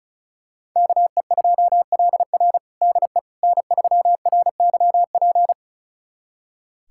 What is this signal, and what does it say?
Morse code, heard as a keyed tone, about 36 wpm, 700 Hz. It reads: KE2LR DE N3RYP
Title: USA Callsign DE Callsign - 5 Characters - 35wpm